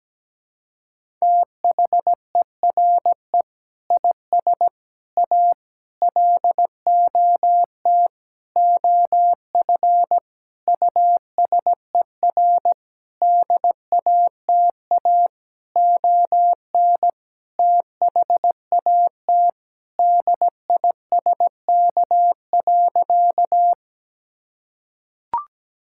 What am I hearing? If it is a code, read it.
Morse code, heard as a keyed tone, about 17 wpm, 700 Hz. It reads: THERE IS A LOT OF USER DATA ON THAT DISK.
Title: Sentences from Top 1000 Words - Review Part 6 - 17wpm